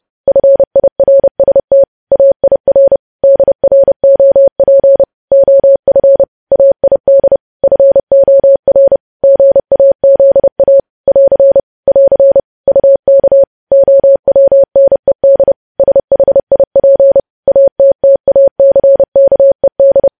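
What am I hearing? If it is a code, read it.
Morse code, heard as a keyed tone, about 30 wpm, 550 Hz. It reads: FIRST AIR DROP OF AID FOR GAZA <AR> <AR> UK OWNED SHIP ATTACKED
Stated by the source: Morse code practice of news headlines at 30.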